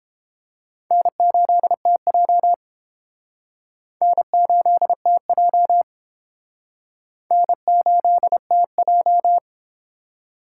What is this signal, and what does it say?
Morse code, heard as a keyed tone, about 29 wpm, 700 Hz. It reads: N8TJ N8TJ N8TJ